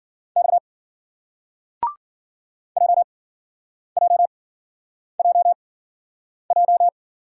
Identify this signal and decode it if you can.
Morse code, heard as a keyed tone, about 50 wpm, spaced as for 9 wpm, 700 Hz. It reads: X JJJJ
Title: Speed-Racing Single Letters 40wpm